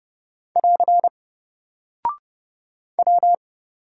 Morse code, heard as a keyed tone, about 28 wpm, 700 Hz. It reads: <AR> W
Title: Single Character (Letter:Number:Punctuation:Prosign) Alt 30wpm